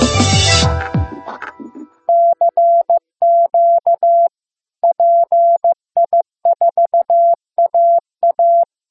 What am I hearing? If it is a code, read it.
Morse code, heard as a keyed tone, about 15 wpm, 680 Hz. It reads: CQ PI4AA
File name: pi4aa / PI4AA Uitzending